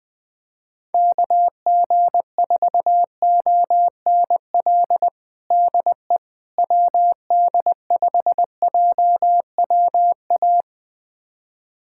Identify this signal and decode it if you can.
Morse code, heard as a keyed tone, about 20 wpm, 700 Hz. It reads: KG4ONL DE WD5JWA
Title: USA Callsign DE Callsign - 6 Characters - 20wpm